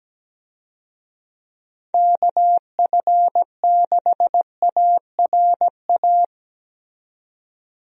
Morse code, heard as a keyed tone, about 17 wpm, 700 Hz. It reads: KF6ARA
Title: USA Call Signs - 6 Characters - 17wpm